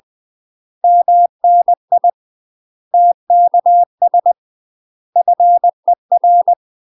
Morse code, heard as a keyed tone, about 20 wpm, 700 Hz. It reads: MNI TKS FER
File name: UK MASTER QSOs TRAINING / 20WPM / UK MASTER QSOs 0003